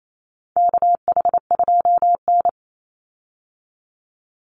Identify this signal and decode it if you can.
Morse code, heard as a keyed tone, about 28 wpm, 700 Hz. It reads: KH2N
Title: Call Signs Repeated 3x - 28wpm